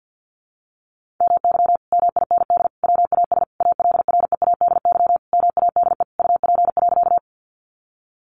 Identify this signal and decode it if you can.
Morse code, heard as a keyed tone, about 50 wpm, 700 Hz. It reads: MY MIND WAS ALREADY MADE UP.
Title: Sentences from Top 700 Words 50wpm